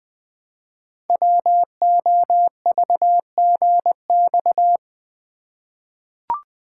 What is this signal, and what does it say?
Morse code, heard as a keyed tone, about 20 wpm, 700 Hz. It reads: WOVGX
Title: Code Groups - 5 Letters - 20wpm